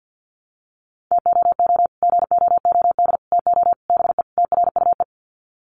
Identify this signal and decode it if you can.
Morse code, heard as a keyed tone, about 48 wpm, 700 Hz. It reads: TOO GOOD TO BE TRUE
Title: Sets of 5 Words Encore 50wpm